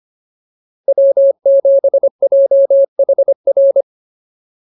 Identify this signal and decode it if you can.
Morse code, heard as a keyed tone, about 25 wpm, 550 Hz. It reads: W7JHR
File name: POTA307a 25